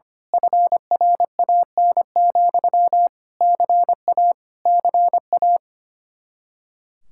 Morse code, heard as a keyed tone, about 24 wpm, 700 Hz. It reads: FRAN, CA CA
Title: Speed-Racing Common QSO Phrases - 25wpm